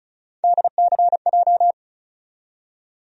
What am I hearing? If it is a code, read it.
Morse code, heard as a keyed tone, about 36 wpm, 700 Hz. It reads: DCJ
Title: USA Call Sign Postfixes - 3 Letters 35wpm